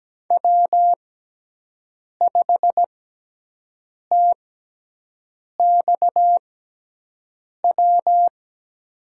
Morse code, heard as a keyed tone, about 17 wpm, spaced as for 6 wpm, 700 Hz. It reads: W5TXW
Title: USA Call Signs - 5 Characters - 6x Character Spacing - 17wpm